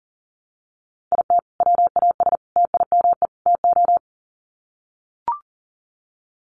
Morse code, heard as a keyed tone, about 40 wpm, 700 Hz. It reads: IT WAS TIME TO
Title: Sets of 4 Words 40wpm